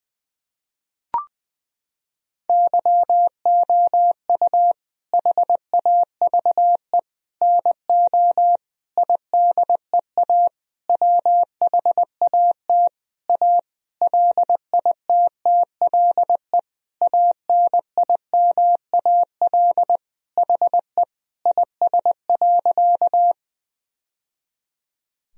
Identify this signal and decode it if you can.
Morse code, heard as a keyed tone, about 21 wpm, 700 Hz. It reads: YOU HAVE NO IDEA WHAT A LITTLE ANIMAL HE IS.
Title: Sentences from Top 1000 Words - Review Part 1 - 20wpm